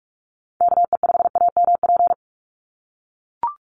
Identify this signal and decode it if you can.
Morse code, heard as a keyed tone, about 44 wpm, 700 Hz. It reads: KE5AMP
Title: USA Call Signs 45wpm